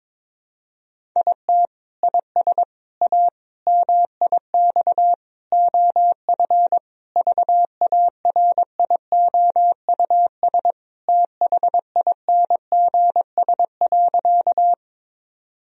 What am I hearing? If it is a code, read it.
Morse code, heard as a keyed tone, about 22 wpm, 700 Hz. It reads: IT IS A MIX OF VARIOUS THINGS.
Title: Sentences from Top 1000 Words - Review Part 4 - 22wpm